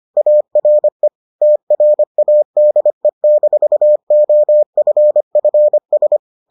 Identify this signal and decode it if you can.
Morse code, heard as a keyed tone, about 25 wpm, 600 Hz. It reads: ARE TRADE-OFFS